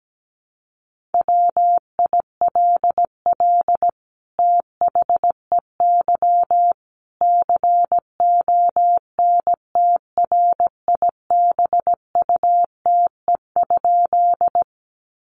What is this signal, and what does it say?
Morse code, heard as a keyed tone, about 17 wpm, 700 Hz. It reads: WILL THEY CONTRIBUTE?
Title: Sentences from Top 1400 Words 17wpm